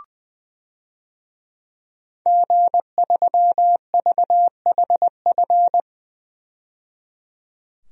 Morse code, heard as a keyed tone, about 20 wpm, 700 Hz. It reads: G3VHF
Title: DXCC-100 Call Signs - 5 Characters - 20wpm